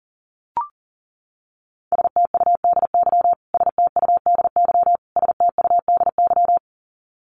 Morse code, heard as a keyed tone, about 40 wpm, 700 Hz. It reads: STUDY STUDY STUDY
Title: Top 300 Words Repeated 40wpm